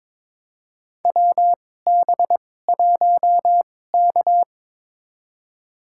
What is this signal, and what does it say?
Morse code, heard as a keyed tone, about 22 wpm, spaced as for 16 wpm, 700 Hz. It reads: WB1K
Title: USA Call Signs - 2x Character Spacing - 22wpm